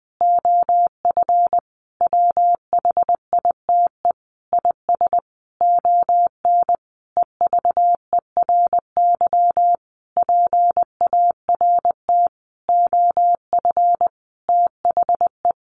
Morse code, heard as a keyed tone, about 19 wpm, 700 Hz. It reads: OF WHITE IS ON EVERY PART OF THE
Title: Sentences from Top 1500 Words 20wpm